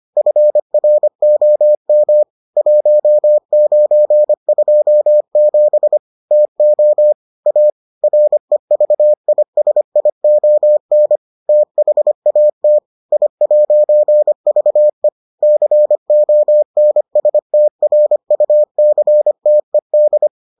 Morse code, heard as a keyed tone, about 25 wpm, 600 Hz. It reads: FROM 1927 TO A REVISION THAT I'VE CONSTRUCTED